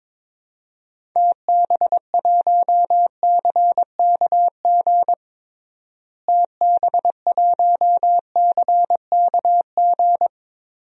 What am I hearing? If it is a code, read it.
Morse code, heard as a keyed tone, about 22 wpm, 700 Hz. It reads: TB1CKG TB1CKG
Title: DXCC-100 Call Signs - 6 Characters - Repeated 3x - 22wpm